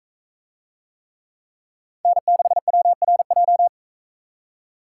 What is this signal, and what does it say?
Morse code, heard as a keyed tone, about 42 wpm, 700 Hz. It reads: N6WRJ